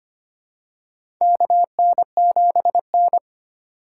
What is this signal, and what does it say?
Morse code, heard as a keyed tone, about 25 wpm, 700 Hz. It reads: KN7N
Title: USA Call Signs - 4x Character Spacing - 25wpm